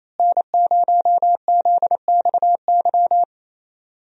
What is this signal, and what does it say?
Morse code, heard as a keyed tone, about 28 wpm, 700 Hz. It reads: N0ZXY
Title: USA Call Signs - 5 Characters - Repeated 3x - 28wpm